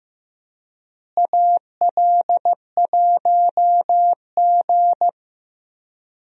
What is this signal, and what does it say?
Morse code, heard as a keyed tone, about 15 wpm, 700 Hz. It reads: AL1G
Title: USA Call Signs - 4 Characters - Repeated 2x - 15wpm